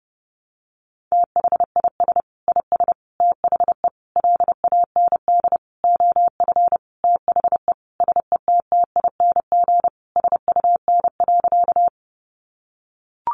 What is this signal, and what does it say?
Morse code, heard as a keyed tone, about 30 wpm, 700 Hz. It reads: THIS IS THE LAND OF THE SETTING SUN.
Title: Sentences from Top 1300 Words 30wpm